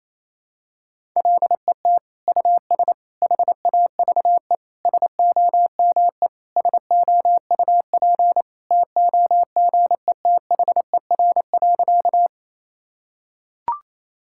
Morse code, heard as a keyed tone, about 27 wpm, 700 Hz. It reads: LET US HAVE SOME SOUP TOGETHER.
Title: Sentences from Top 1000 Words - Review Part 6 - 28wpm